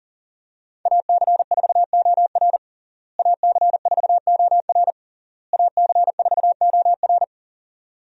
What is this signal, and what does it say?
Morse code, heard as a keyed tone, about 40 wpm, 700 Hz. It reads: AC4OR AC4OR AC4OR
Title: USA Call Signs - 5 Characters - Repeated 3x - 40wpm